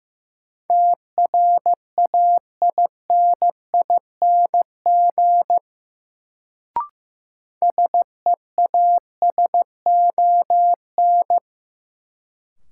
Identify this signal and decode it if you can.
Morse code, heard as a keyed tone, about 15 wpm, 700 Hz. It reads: TRAINING SEASON
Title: Sentences from Top 1400 Words - New Word Drill Alt - 15wpm